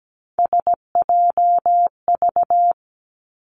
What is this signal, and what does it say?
Morse code, heard as a keyed tone, about 17 wpm, 700 Hz. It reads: SJV
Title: ARRL Sections 17wpm